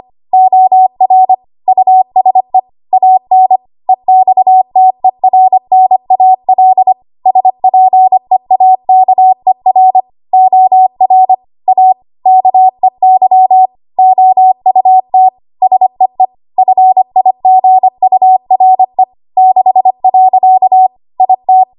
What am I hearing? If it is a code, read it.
Morse code, heard as a keyed tone, about 25 wpm, 750 Hz. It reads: OR USE AN EXTERNAL SPEAKER OR A KEY OUT SEE FIGURE 6. IT
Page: 25 WPM Code Archive